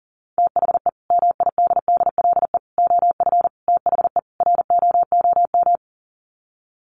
Morse code, heard as a keyed tone, about 40 wpm, 700 Hz. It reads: THE MIDDLE OF THE ROOM
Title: Sets of 5 Words Encore2 40wpm